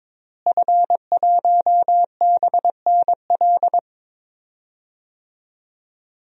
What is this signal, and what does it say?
Morse code, heard as a keyed tone, about 22 wpm, 700 Hz. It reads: F1BNL
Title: DXCC-100 Call Signs - 5 Characters - Repeated 2x - 22wpm